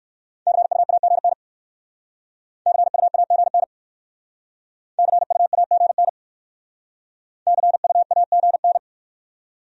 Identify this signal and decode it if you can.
Morse code, heard as a keyed tone, about 56 wpm, 700 Hz. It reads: CUAGN CUAGN CUAGN CUAGN